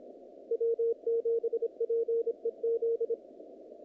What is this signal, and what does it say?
Morse code, heard as a keyed tone, about 26 wpm, 460 Hz. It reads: W7PEZ
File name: contests / naqp-cw-k5tr-op-k5pi- -2200